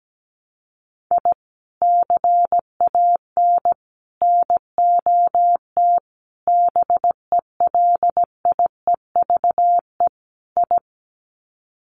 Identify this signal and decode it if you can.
Morse code, heard as a keyed tone, about 17 wpm, 700 Hz. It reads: I CAN NOT BELIEVE I
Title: Sets of 5 Words Encore2 17wpm